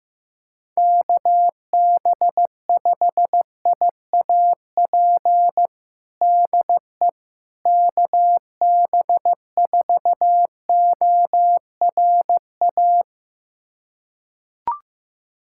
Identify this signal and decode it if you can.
Morse code, heard as a keyed tone, about 16 wpm, 700 Hz. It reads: KB5IAP DE KB4ORA
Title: USA Callsign DE Callsign - 6 Characters - 15wpm